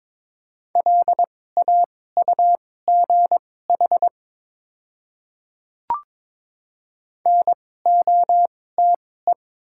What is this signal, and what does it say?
Morse code, heard as a keyed tone, about 22 wpm, spaced as for 16 wpm, 700 Hz. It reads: LAUGH NOTE
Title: 2x Character Spacing - Top 500 Words - 22wpm